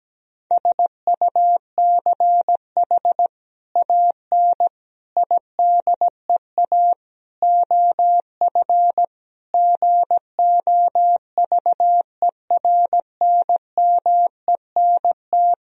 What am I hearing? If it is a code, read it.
Morse code, heard as a keyed tone, about 17 wpm, 700 Hz. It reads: SUCH AN IDEA OF GOVERNMENT
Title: Sentences from Top 1000 Words - Review Part 3 - 17wpm